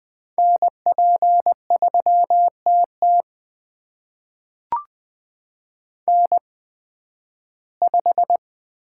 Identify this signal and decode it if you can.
Morse code, heard as a keyed tone, about 20 wpm, spaced as for 13 wpm, 700 Hz. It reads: NP3TT N5
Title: USA Call Signs - 5 Characters - 8x Character Spacing - 20wpm